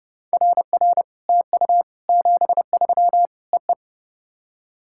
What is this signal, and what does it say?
Morse code, heard as a keyed tone, about 30 wpm, 700 Hz. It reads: RR TU 73 EE
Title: SOTA QSO Practice 30wpm